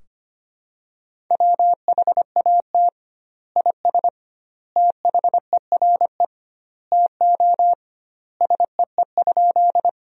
Morse code, heard as a keyed tone, about 25 wpm, 700 Hz. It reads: WHAT IS THERE TO SEE?
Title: 2x Word-Spacing - Sentences from Top 100 Words - 25wpm